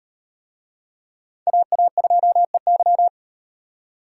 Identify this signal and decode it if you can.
Morse code, heard as a keyed tone, about 38 wpm, 700 Hz. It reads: AA2EY